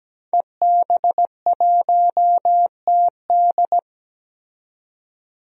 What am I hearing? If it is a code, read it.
Morse code, heard as a keyed tone, about 17 wpm, 700 Hz. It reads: EB1TD
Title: DXCC-100 Call Signs - 5 Characters - 17wpm